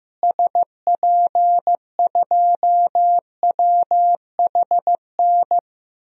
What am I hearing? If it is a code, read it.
Morse code, heard as a keyed tone, about 15 wpm, 700 Hz. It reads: SP2WHN
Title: DXCC-100 Call Signs - 6 Characters - Repeated 2x - 15wpm